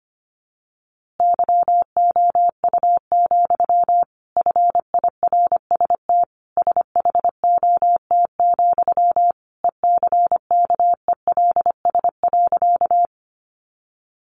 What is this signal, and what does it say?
Morse code, heard as a keyed tone, about 25 wpm, 700 Hz. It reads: YOU, FIRST SHOT, ECKELS.
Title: A Sound Of Thunder 25wpm